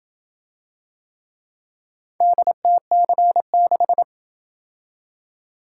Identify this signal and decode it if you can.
Morse code, heard as a keyed tone, about 28 wpm, 700 Hz. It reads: DTC6